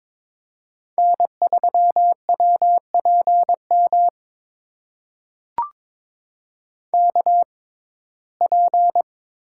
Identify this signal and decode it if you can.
Morse code, heard as a keyed tone, about 22 wpm, spaced as for 16 wpm, 700 Hz. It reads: N3WPM KP
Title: USA Call Signs - 5 Characters - 6x Character Spacing - 22wpm